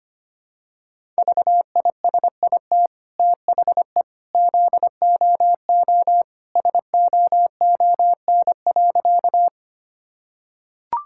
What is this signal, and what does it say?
Morse code, heard as a keyed tone, about 25 wpm, 700 Hz. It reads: VISIT THE ZOO SOON.
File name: Easy Sentences with Hints - 25wpm